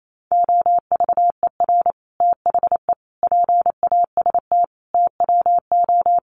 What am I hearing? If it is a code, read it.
Morse code, heard as a keyed tone, about 28 wpm, 700 Hz. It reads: OVER THE PAST TWO
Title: Sets of 5 Words Encore 28wpm